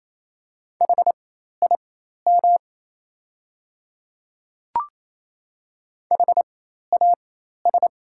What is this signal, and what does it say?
Morse code, heard as a keyed tone, about 27 wpm, spaced as for 13 wpm, 700 Hz. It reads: HIM HAS
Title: 4x Character Spacing - Top 300 Words - 28wpm